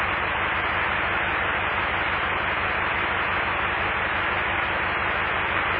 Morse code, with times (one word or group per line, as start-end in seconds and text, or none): none